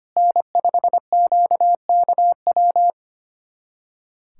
Morse code, heard as a keyed tone, about 25 wpm, 700 Hz. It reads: N5QKW
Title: USA Callsign DE Callsign - 5 Characters - 25wpm